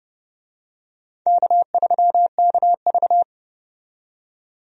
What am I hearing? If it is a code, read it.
Morse code, heard as a keyed tone, about 30 wpm, 700 Hz. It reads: K3KV
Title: USA Call Signs - 4 Characters - 30wpm